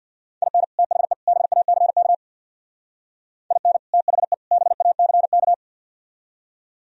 Morse code, heard as a keyed tone, about 62 wpm, 700 Hz. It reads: IN THE BACK IN THE BACK
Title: Speed-Racing - Sets of 3 Words 50wpm